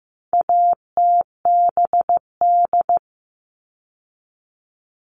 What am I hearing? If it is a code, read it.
Morse code, heard as a keyed tone, about 15 wpm, 700 Hz. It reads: ATBD
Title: All Cut Numbers - 4-Digit 15wpm